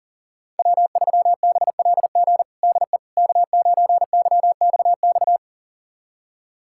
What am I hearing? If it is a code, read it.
Morse code, heard as a keyed tone, about 40 wpm, 700 Hz. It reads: W3BLG DE K9YXX
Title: USA Callsign DE Callsign - 5 Characters - 40wpm